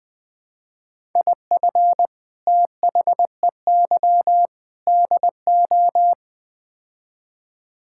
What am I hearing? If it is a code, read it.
Morse code, heard as a keyed tone, about 20 wpm, 700 Hz. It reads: IF THEY DO
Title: Build A Phrase - 4-Word Phrases - 20wpm